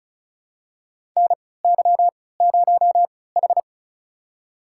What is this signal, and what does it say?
Morse code, heard as a keyed tone, about 36 wpm, spaced as for 20 wpm, 700 Hz. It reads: NY0H
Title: USA Call Signs - 4 Characters - 3x Character Spacing - 35wpm